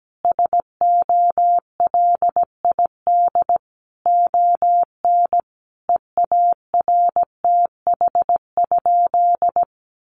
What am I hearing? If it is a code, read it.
Morse code, heard as a keyed tone, about 17 wpm, 700 Hz. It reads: SOLID ON EARTH?
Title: Sentences from Top 1500 Words 17wpm